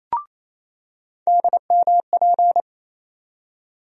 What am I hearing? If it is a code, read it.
Morse code, heard as a keyed tone, about 28 wpm, 700 Hz. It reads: DMP
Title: USA Call Sign Postfixes - 3 Letters 28wpm